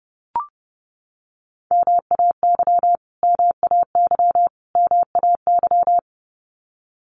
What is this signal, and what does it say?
Morse code, heard as a keyed tone, about 30 wpm, 700 Hz. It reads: MAY MAY MAY
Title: Top 200 Words Repeated 30wpm